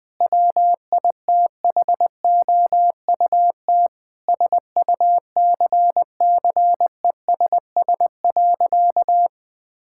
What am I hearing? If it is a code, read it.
Morse code, heard as a keyed tone, about 20 wpm, 700 Hz. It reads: WITHOUT SUCCESS.
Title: Sentences from Top 1000 Words - Review Part 6 - 20wpm